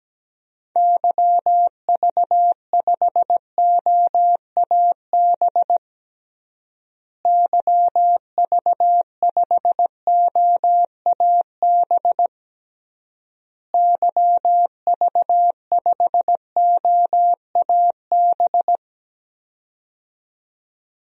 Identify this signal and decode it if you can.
Morse code, heard as a keyed tone, about 17 wpm, 700 Hz. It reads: YV5OAB YV5OAB YV5OAB
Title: DXCC-100 Call Signs - 6 Characters - Repeated 3x - 17wpm